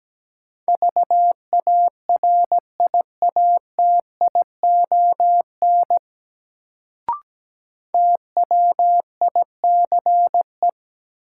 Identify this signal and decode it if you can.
Morse code, heard as a keyed tone, about 17 wpm, 700 Hz. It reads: VARIATION TWICE
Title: Sentences from Top 1600 Words - New Word Drill - 17wpm